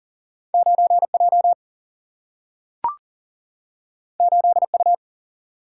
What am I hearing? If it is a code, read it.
Morse code, heard as a keyed tone, about 40 wpm, 700 Hz. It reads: 9 J 8 U
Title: Single-Letter Single-Number 40wpm